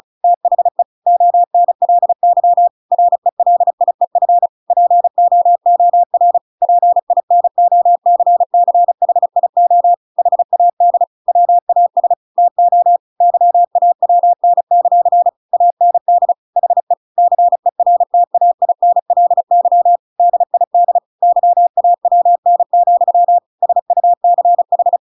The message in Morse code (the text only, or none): THE ONLY RELIEF POOR PINOCCHIO HAD WAS TO YAWN; AND HE CERTAINLY DID YAWN, SUCH